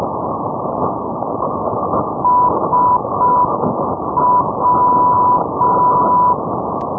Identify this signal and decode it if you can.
Morse code, heard as a keyed tone, about 5 wpm, 990 Hz, with noise. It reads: SW
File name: europe / ukr / ukr-400 ndb shepetivka sw-830-net-moskva-2012-rb